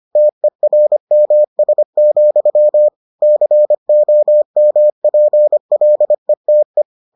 Morse code, heard as a keyed tone, about 25 wpm, 600 Hz. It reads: TERMS, COMPLETE